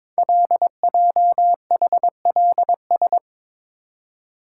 Morse code, heard as a keyed tone, about 23 wpm, 700 Hz. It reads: LJHLS
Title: Code Groups - 5 Letters - 22wpm